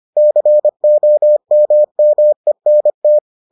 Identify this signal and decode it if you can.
Morse code, heard as a keyed tone, about 25 wpm, 600 Hz. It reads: COMMENT